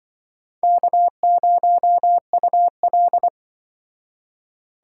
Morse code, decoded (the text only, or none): K0UL